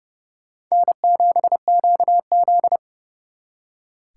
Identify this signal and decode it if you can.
Morse code, heard as a keyed tone, about 36 wpm, 700 Hz. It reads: N7QZ
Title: Speed Racing - USA Call Signs 30wpm